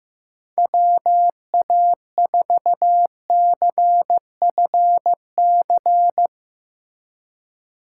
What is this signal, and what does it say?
Morse code, heard as a keyed tone, about 15 wpm, 700 Hz. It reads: WA4CFC
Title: USA Call Signs - 6 Characters - 4x Character Spacing - 15wpm